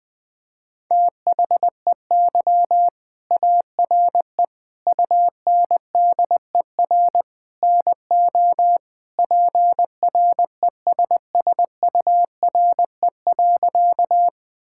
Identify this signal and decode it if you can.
Morse code, heard as a keyed tone, about 19 wpm, 700 Hz. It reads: THEY ARE UNDER NO PRESSURE.
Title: Sentences from Top 1000 Words - Review Part 6 - 20wpm